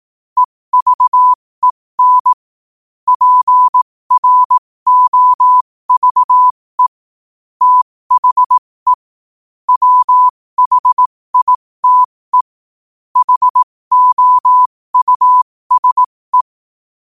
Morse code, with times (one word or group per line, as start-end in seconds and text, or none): 0.4-2.3 EVEN
3.1-6.9 PROVE
7.6-8.9 THE
9.7-12.4 WHITE
13.2-16.4 HOUSE